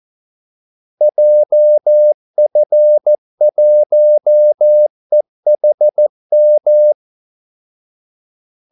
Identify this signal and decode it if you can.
Morse code, heard as a keyed tone, about 14 wpm, 600 Hz. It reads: JF1EHM
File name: Callsigns-14wpm